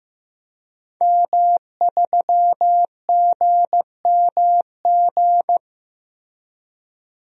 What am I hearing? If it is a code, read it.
Morse code, heard as a keyed tone, about 15 wpm, 700 Hz. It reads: M3GMG
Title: DXCC-100 Call Signs - 5 Characters - Repeated 3x - 15wpm